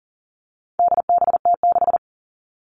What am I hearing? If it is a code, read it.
Morse code, heard as a keyed tone, about 40 wpm, 700 Hz. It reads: DBT6
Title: All Cut Numbers - 4-Digit 40wpm